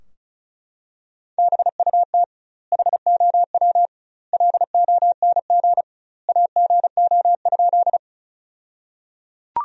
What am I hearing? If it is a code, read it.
Morse code, heard as a keyed tone, about 34 wpm, 700 Hz. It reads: BUT HOW LONG AGO?
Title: 2x Word-Spacing - Sentences from Top 500 Words - 35wpm